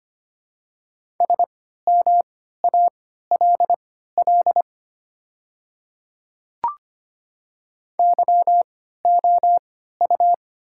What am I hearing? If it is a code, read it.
Morse code, heard as a keyed tone, about 25 wpm, spaced as for 14 wpm, 700 Hz. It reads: SMALL YOU
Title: 3x Character Spacing - Top 200 Words - 25wpm